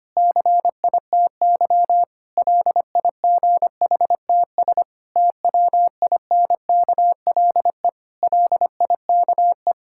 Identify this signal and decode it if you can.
Morse code, heard as a keyed tone, about 25 wpm, 700 Hz. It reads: CITY LIGHTS TWINKLE LIKE
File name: practice / 25wpm / Difficult Sentences with Hints - 25wpm